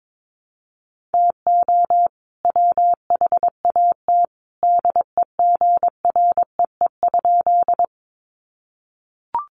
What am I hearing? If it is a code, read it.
Morse code, heard as a keyed tone, about 22 wpm, 700 Hz. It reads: TO WHAT DEGREE?
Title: Sentences from Top 1500 Words 22wpm